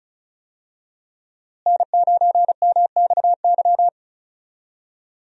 Morse code, heard as a keyed tone, about 36 wpm, 700 Hz. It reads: N9MXY
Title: USA Call Signs - 5 Characters - Repeated 3x - 35wpm